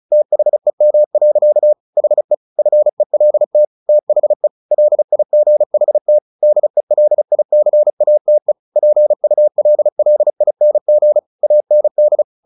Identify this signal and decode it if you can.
Morse code, heard as a keyed tone, about 34 wpm, 600 Hz. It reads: THEM. HE FELT THE LIGHT DELICATE PULLING AND
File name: the old man and the sea / the old man and the sea 1 35wpm